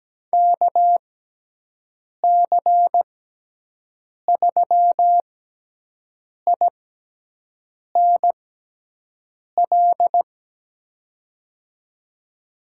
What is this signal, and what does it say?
Morse code, heard as a keyed tone, about 17 wpm, spaced as for 6 wpm, 700 Hz. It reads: KC3INL
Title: USA Call Signs - 6 Characters - 6x Character Spacing - 17wpm